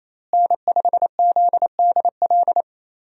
Morse code, heard as a keyed tone, about 28 wpm, 700 Hz. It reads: N5ZDL
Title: USA Call Signs - 5 Characters - Repeated 3x - 28wpm